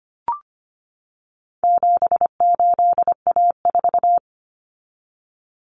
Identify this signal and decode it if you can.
Morse code, heard as a keyed tone, about 25 wpm, 700 Hz. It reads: 78A4
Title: Common Cut Numbers - 4-Digit 25wpm